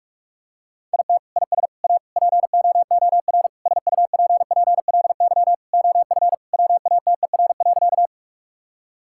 Morse code, heard as a keyed tone, about 46 wpm, 700 Hz. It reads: IT IS A POOR SUPPLY OF WATER.
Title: Sentences from Top 1000 Words - Review Part 5 - 45wpm